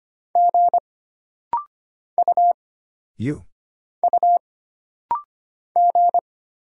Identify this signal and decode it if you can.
Morse code, heard as a keyed tone, about 25 wpm, 700 Hz. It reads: G U U G
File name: Single Character (Letter:Number:Punctuation:Prosign) - Rapid-Fire - 25wpm